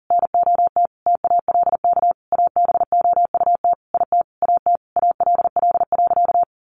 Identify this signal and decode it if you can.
Morse code, heard as a keyed tone, about 37 wpm, 700 Hz. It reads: NOT TALK ABOUT IT AT ALL.
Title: Sentences from Top 800 Words 40wpm